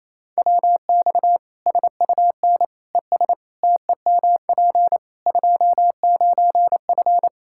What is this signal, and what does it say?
Morse code, heard as a keyed tone, about 28 wpm, 700 Hz. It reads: WX SUN ES TEMP 29F
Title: K1USN SST QSO Practice 28wpm